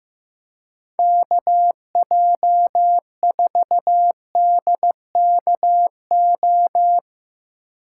Morse code, heard as a keyed tone, about 16 wpm, 700 Hz. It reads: KJ4DKO
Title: Speed Racing - USA 2x3 Call Signs 15wpm